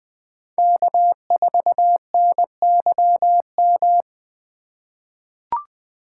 Morse code, heard as a keyed tone, about 20 wpm, 700 Hz. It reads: K4NYM
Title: 2023 POTA Activator Call Signs - 20wpm